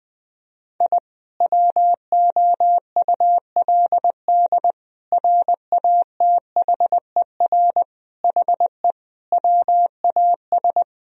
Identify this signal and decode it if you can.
Morse code, heard as a keyed tone, about 21 wpm, 700 Hz. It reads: I WOULD RATHER HE WAS
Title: Sentences from Top 1000 Words - Review Part 4 - 20wpm